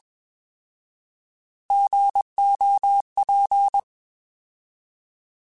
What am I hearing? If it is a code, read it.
Morse code, heard as a keyed tone, about 21 wpm, 800 Hz. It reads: GOP
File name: Alphabetical / Alphabetical mp3s / Gs @20